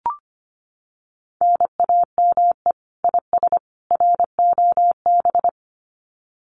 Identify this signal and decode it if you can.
Morse code, heard as a keyed tone, about 25 wpm, 700 Hz. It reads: NAME IS ROB
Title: Common QSO Phrases 25wpm